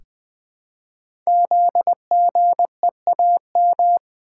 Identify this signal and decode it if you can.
Morse code, heard as a keyed tone, about 20 wpm, 700 Hz. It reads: ZGEAM
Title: Code Groups - 5 Letters - 20wpm